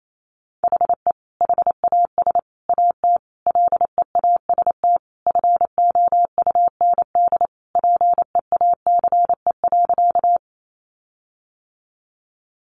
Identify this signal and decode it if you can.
Morse code, heard as a keyed tone, about 28 wpm, 700 Hz. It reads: HE HAS AT LEAST FOUND PEACE.
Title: Sentences from Top 1500 Words 28wpm